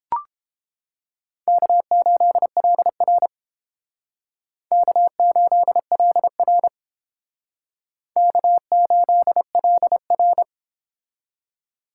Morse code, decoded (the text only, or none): K8LR K8LR K8LR